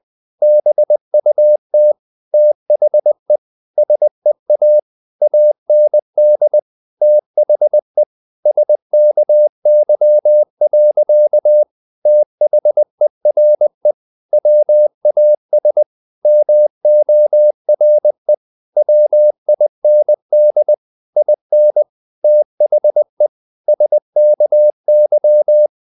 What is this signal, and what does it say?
Morse code, heard as a keyed tone, about 20 wpm, 600 Hz. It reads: BUT THE SEA AND THE SKY. THERE WAS MORE WIND IN THE SKY
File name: the old man and the sea part 2 20wpm